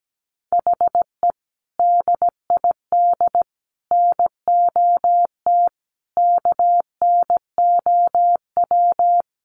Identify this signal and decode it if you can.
Morse code, heard as a keyed tone, about 17 wpm, 700 Hz. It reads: HE DID NOT KNOW
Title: Sentences from Top 1500 Words 17wpm